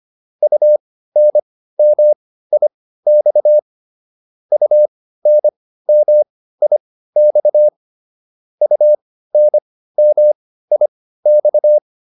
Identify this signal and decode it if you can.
Morse code, heard as a keyed tone, about 25 wpm, spaced as for 15 wpm, 600 Hz. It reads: UNMIX UNMIX UNMIX